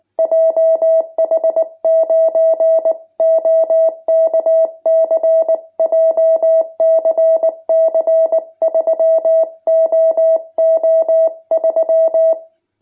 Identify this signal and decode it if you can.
Morse code, heard as a keyed tone, about 19 wpm, 650 Hz. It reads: JH9OKC JCC 3OO3
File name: jh8cbh / 2021cw